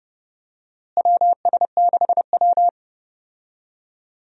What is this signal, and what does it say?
Morse code, heard as a keyed tone, about 30 wpm, 700 Hz. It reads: WS6W
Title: USA Call Signs - 4 Characters - Repeated 3x - 30wpm